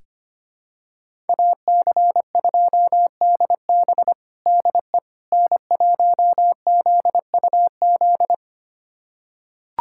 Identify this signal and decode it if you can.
Morse code, heard as a keyed tone, about 25 wpm, 700 Hz. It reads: AC2DB DE N1ZUZ
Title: USA Callsign DE Callsign - 5 Characters - 25wpm